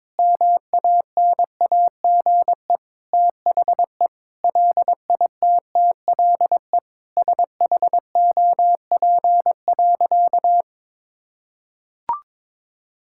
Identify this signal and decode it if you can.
Morse code, heard as a keyed tone, about 22 wpm, 700 Hz. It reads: MANAGE THE LITTLE SHOP.
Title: Sentences from Top 1600 Words 22wpm